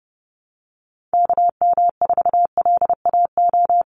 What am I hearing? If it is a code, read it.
Morse code, heard as a keyed tone, about 29 wpm, 700 Hz. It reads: KM4LAO
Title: Call Signs Repeated 3x - 30wpm